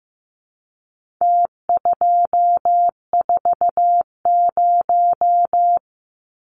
Morse code, heard as a keyed tone, about 15 wpm, 700 Hz. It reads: T240
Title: Numbers - Contest Style - 4-Digit 15wpm